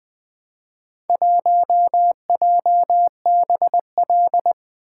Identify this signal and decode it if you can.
Morse code, heard as a keyed tone, about 20 wpm, 700 Hz. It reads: 1JBL
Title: Four Letter-Number Combinations - 20wpm